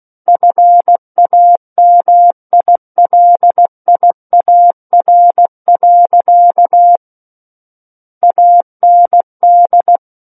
Morse code, heard as a keyed tone, about 16 wpm, 700 Hz. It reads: FAMILIAR. AND
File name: Book One / 07-Chapter 7 - 16 WPM